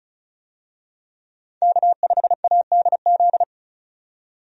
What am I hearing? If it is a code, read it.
Morse code, heard as a keyed tone, about 36 wpm, 700 Hz. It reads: K5ADZ
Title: USA Call Signs - 5 Characters - 35wpm